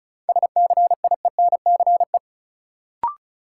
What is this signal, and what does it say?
Morse code, heard as a keyed tone, about 36 wpm, 700 Hz. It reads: SCIENCE
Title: Sentences from Top 1000 Words - Review Part 2 - New Word Drill - 35wpm